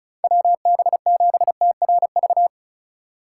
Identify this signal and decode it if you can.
Morse code, heard as a keyed tone, about 36 wpm, 700 Hz. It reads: WB7TRV
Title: USA Call Signs - 6 Characters - 2x Character Spacing - 35wpm